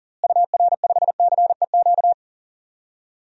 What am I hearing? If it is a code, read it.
Morse code, heard as a keyed tone, about 37 wpm, 700 Hz. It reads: UR5CEQ
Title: DXCC-100 Call Signs - 6 Characters - Repeated 2x - 40wpm